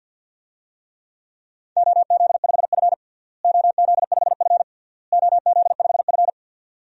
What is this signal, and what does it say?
Morse code, heard as a keyed tone, about 50 wpm, 700 Hz. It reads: OZ5F OZ5F OZ5F
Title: DXCC-100 Call Signs - 4 Characters - Repeated 3x - 50wpm